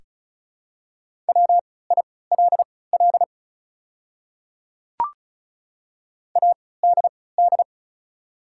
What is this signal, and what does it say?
Morse code, heard as a keyed tone, about 36 wpm, spaced as for 20 wpm, 700 Hz. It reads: WILL ADD
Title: 3x Character Spacing - Top 200 Words - 35wpm